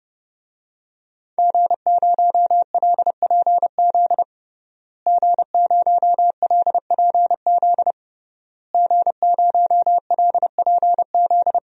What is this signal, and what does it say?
Morse code, heard as a keyed tone, about 29 wpm, 700 Hz. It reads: G0LPZ G0LPZ G0LPZ
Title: DXCC-100 Call Signs - 5 Characters - Repeated 3x - 30wpm